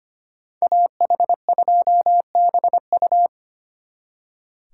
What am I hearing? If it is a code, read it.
Morse code, heard as a keyed tone, about 30 wpm, 700 Hz. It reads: AH2BU
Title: Speed Racing - USA 2x2 Call Signs 25wpm